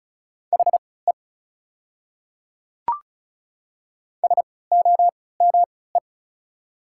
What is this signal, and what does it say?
Morse code, heard as a keyed tone, about 33 wpm, spaced as for 19 wpm, 690 Hz. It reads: HE SOME
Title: 3x Character Spacing - Top 100 Words - 35wpm